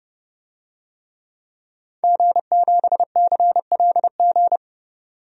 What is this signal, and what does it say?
Morse code, heard as a keyed tone, about 29 wpm, 700 Hz. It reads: G7CLG
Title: DXCC-100 Call Signs - 5 Characters - 30wpm